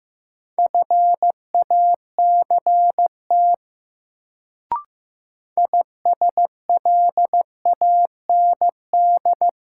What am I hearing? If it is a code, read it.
Morse code, heard as a keyed tone, about 15 wpm, 700 Hz. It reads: FACT ISLAND
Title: Sentences from Top 500 Words - New Word Drill - 15wpm